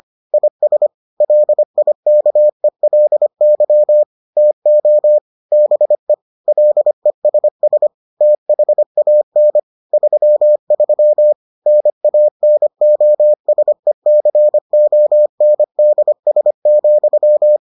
Morse code, heard as a keyed tone, about 25 wpm, 600 Hz. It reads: IS LIKELY TO BE LESS THAN 33 NANOSECONDS,